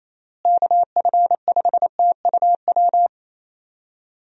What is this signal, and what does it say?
Morse code, heard as a keyed tone, about 28 wpm, 700 Hz. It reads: KF5TUW
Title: USA Call Signs - 6 Characters - Repeated 3x - 28wpm